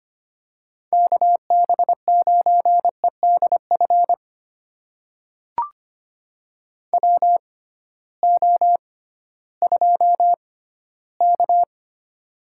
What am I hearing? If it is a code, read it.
Morse code, heard as a keyed tone, about 25 wpm, spaced as for 15 wpm, 700 Hz. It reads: KB9EDF WO2K
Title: USA Call Signs - 6x Character Spacing - 25wpm